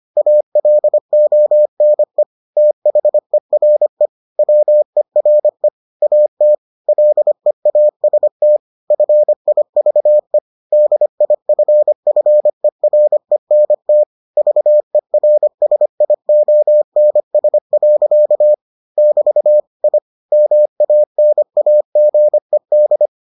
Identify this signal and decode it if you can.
Morse code, heard as a keyed tone, about 25 wpm, 600 Hz. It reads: ALONE THERE WERE AT LEAST FIVE DIFFERENT VERSIONS. <BT> I MANAGED